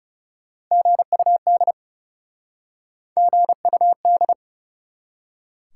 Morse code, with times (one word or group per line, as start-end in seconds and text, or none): 0.7-1.7 GUD
3.2-4.3 GUD